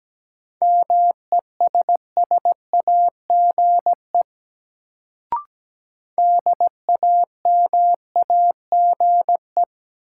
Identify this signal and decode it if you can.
Morse code, heard as a keyed tone, about 17 wpm, 700 Hz. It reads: MESSAGE DAMAGE
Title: Sentences from Top 1100 Words - New Word Drill Alt - 17wpm